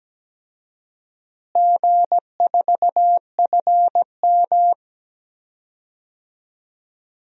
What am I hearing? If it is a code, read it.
Morse code, heard as a keyed tone, about 17 wpm, 700 Hz. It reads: G4FM
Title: DXCC-100 Call Signs - 4 Characters - Repeated 2x - 17wpm